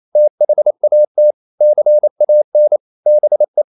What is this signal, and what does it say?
Morse code, heard as a keyed tone, about 28 wpm, 600 Hz. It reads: THAT CAN BE
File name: CW / Practice / EnglishQuotations0128